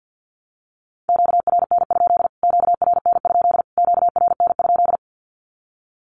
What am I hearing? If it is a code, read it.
Morse code, heard as a keyed tone, about 50 wpm, 700 Hz. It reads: QRN? QRN? QRN?
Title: Q-Codes Repeated 3x 50wpm